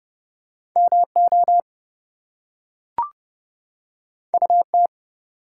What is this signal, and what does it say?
Morse code, heard as a keyed tone, about 30 wpm, 700 Hz. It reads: MO UT
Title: US States and Canadian Provinces 30wpm